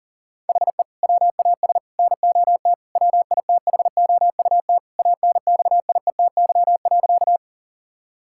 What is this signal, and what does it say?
Morse code, heard as a keyed tone, about 40 wpm, 700 Hz. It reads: HE WAS NOT WITHOUT ANXIETY.
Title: Sentences from Top 1600 Words 40wpm